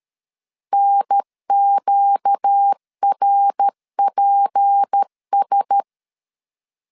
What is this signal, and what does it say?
Morse code, heard as a keyed tone, about 13 wpm, 800 Hz. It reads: NQRPS
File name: LEZIONI / 6 PKX0 / 6 PK 12wpm 2006